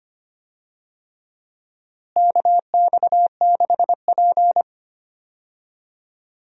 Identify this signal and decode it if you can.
Morse code, heard as a keyed tone, about 25 wpm, 700 Hz. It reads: KX6P
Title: USA Call Signs - 4 Characters - Repeated 3x - 25wpm